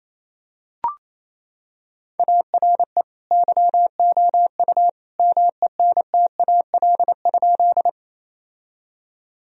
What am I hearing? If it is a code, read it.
Morse code, heard as a keyed tone, about 27 wpm, 700 Hz. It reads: ARE YOU MENTAL?
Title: Sentences from Top 1000 Words - Review Part 4 - 28wpm